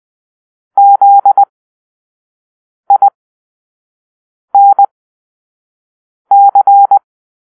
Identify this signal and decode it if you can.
Morse code, heard as a keyed tone, about 20 wpm, spaced as for 5 wpm, 800 Hz. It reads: ZINC
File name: arrl / Grp-5 Drill-4